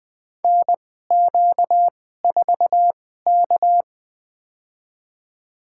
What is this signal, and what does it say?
Morse code, heard as a keyed tone, about 20 wpm, spaced as for 14 wpm, 700 Hz. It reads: NQ4K
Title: USA Call Signs - 4 Characters - 2x Character Spacing - 20wpm